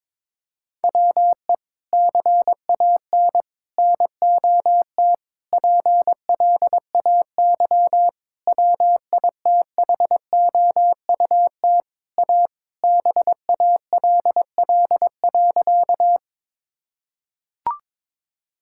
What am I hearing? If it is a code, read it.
Morse code, heard as a keyed tone, about 22 wpm, 700 Hz. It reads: WE CAN NOT PLAY WITHOUT A BALL.
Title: Sentences from Top 1000 Words - Review Part 6 - 22wpm